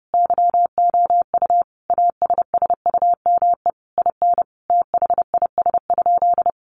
Morse code, heard as a keyed tone, about 30 wpm, 700 Hz. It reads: YOU ASSUME IN THIS?
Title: Sentences from Top 1500 Words 30wpm